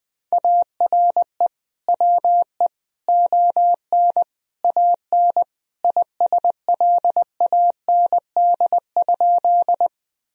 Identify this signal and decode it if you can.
Morse code, heard as a keyed tone, about 21 wpm, 700 Hz. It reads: ARE WE ON AN ISLAND?
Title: Sentences from Top 1000 Words - Review Part 3 - 20wpm